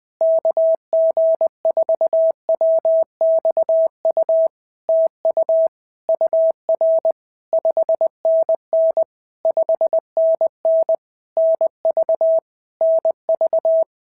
Morse code, heard as a keyed tone, about 20 wpm, 650 Hz. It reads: KG4WXU TU UR 5NN 5NN NV NV
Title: POTA QSO Practice - 20wpm